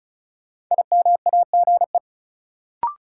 Morse code, heard as a keyed tone, about 36 wpm, 700 Hz. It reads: IMAGE
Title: Sentences from Top 700 Words - New Word Drill - 35wpm